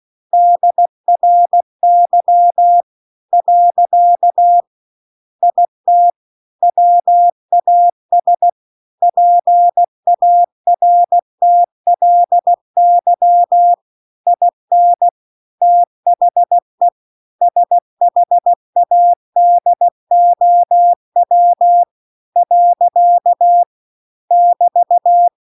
Morse code, heard as a keyed tone, about 16 wpm, 700 Hz. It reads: DRY . IT WAS PARTLY IN THE SHADOW . <BT>